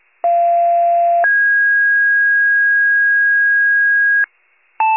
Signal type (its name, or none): two-tone page